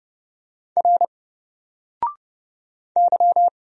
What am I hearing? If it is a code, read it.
Morse code, heard as a keyed tone, about 30 wpm, 700 Hz. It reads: R Y
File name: Single-Letter-Number-30wpm